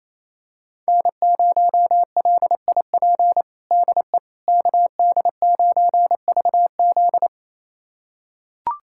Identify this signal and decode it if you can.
Morse code, heard as a keyed tone, about 28 wpm, 690 Hz. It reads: N0LIP DE KD9VZ
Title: USA Callsign DE Callsign - 5 Characters - 28wpm